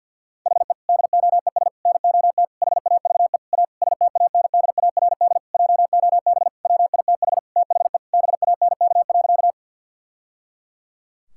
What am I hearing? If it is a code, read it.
Morse code, heard as a keyed tone, about 52 wpm, 700 Hz. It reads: HE DOES NOT HAVE A STANDARD JOB WITH THE BANK.
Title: Sentences from Top 1000 Words - Review Part 6 - 50wpm